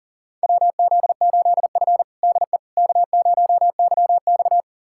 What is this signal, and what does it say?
Morse code, heard as a keyed tone, about 40 wpm, 700 Hz. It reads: WZ8F DE K0YX
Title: USA Callsign DE Callsign - 4 Characters - 40wpm